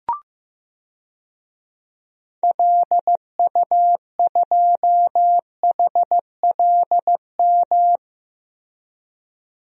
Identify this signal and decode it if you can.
Morse code, heard as a keyed tone, about 15 wpm, 700 Hz. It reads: LU2HLM
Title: DXCC-100 Call Signs - 6 Characters - Repeated 3x - 15wpm